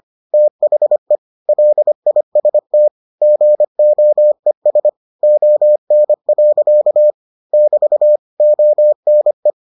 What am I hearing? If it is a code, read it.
Morse code, heard as a keyed tone, about 25 wpm, 600 Hz. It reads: THE LIST GOES ON. <BT> ONE